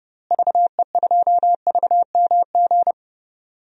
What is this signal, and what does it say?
Morse code, heard as a keyed tone, about 30 wpm, 700 Hz. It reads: VE2VMG
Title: DXCC-100 Call Signs - 6 Characters - Repeated 2x - 30wpm